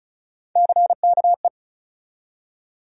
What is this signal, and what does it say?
Morse code, heard as a keyed tone, about 34 wpm, 700 Hz. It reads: CKE
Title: USA Call Sign Postfixes - 3 Letters 35wpm